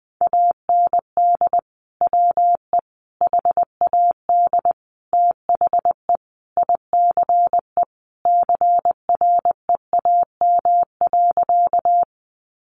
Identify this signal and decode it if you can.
Morse code, heard as a keyed tone, about 20 wpm, 700 Hz. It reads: AND WE HAD THE ICE CREAM.
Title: Sentences from Top 1500 Words 20wpm